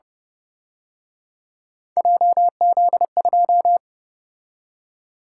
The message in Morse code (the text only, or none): JZ2